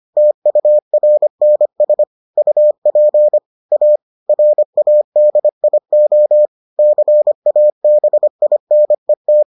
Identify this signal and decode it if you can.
Morse code, heard as a keyed tone, about 25 wpm, 600 Hz. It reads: TURNS UP A RADIO CABINET